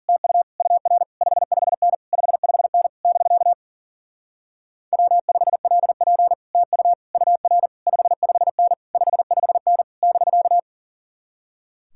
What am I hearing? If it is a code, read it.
Morse code, heard as a keyed tone, about 48 wpm, 700 Hz. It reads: TU UR 55N 55N <BK> W5LP TU UR 55N 55N <BK>
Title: Speed-Racing Common QSO Phrases - 40wpm